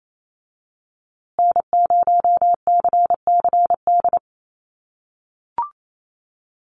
Morse code, heard as a keyed tone, about 28 wpm, 700 Hz. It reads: N0CCD
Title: USA Call Signs Repeated 3x 28wpm